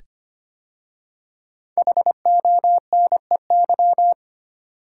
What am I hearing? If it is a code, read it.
Morse code, heard as a keyed tone, about 25 wpm, 700 Hz. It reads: HONEY
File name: 5 Letter Words - First 3 Letters Given - 25wpm